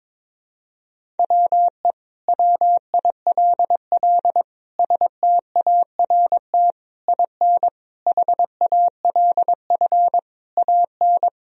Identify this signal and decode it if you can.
Morse code, heard as a keyed tone, about 22 wpm, 700 Hz. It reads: WE WILL START IN HALF AN
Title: Sentences from Top 1000 Words - Review Part 1 - 22wpm